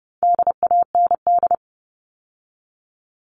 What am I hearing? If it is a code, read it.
Morse code, heard as a keyed tone, about 30 wpm, 700 Hz. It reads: DAND
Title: All Cut Numbers - 4-Digit 30wpm